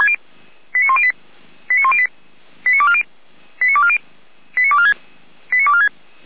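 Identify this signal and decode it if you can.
Five-tone selective call (ZVEI), heard as quick runs of five tones, tones of about 70 ms. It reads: 89108 89108 89240 89240 89246 89246
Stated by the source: ZVEI five-tone call received with an SDR, recorded for Feuerwehr Orlamünde.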